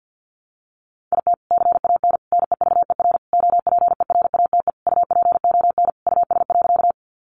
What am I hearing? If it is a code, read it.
Morse code, heard as a keyed tone, about 50 wpm, 700 Hz. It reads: IT CAN NEVER OPERATE UPON US.
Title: Sentences from Top 1500 Words 50wpm